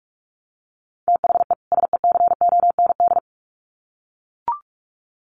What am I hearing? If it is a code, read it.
Morse code, heard as a keyed tone, about 44 wpm, 700 Hz. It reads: THE SECOND
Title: Sets of 2 Words 45wpm